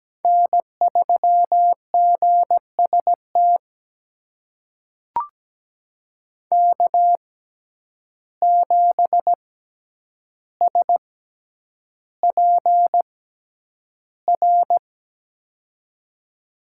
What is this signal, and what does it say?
Morse code, heard as a keyed tone, about 17 wpm, spaced as for 9 wpm, 700 Hz. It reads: N3GST K7SPR
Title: USA Call Signs - 5 Characters - 6x Character Spacing - 17wpm